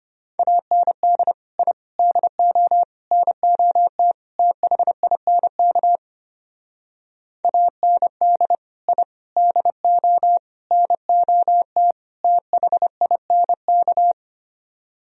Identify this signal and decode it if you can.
Morse code, heard as a keyed tone, about 30 wpm, 700 Hz. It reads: AND I DO NOT THINK AND I DO NOT THINK
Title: Speed-Racing - Sets of 5 Words 25wpm